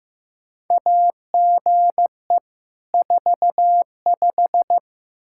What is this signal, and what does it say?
Morse code, heard as a keyed tone, about 15 wpm, 700 Hz. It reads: AGE 45
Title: Practice QSO - Exchange 3 - More Info - 15wpm